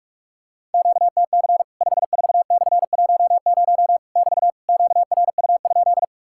Morse code, heard as a keyed tone, about 46 wpm, 700 Hz. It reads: QTC 54/10 <BT> QRU?
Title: Worked All Europe DX Contest - QTC Exchange - 45wpm